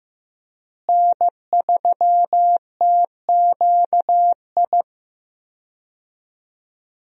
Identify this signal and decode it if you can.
Morse code, heard as a keyed tone, about 15 wpm, 700 Hz. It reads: N3TQI
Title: USA Call Signs - 5 Characters - Repeated 2x - 15wpm